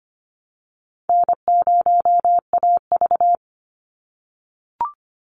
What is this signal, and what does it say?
Morse code, heard as a keyed tone, about 25 wpm, 700 Hz. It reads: N0AV
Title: Easy Call Signs - 25wpm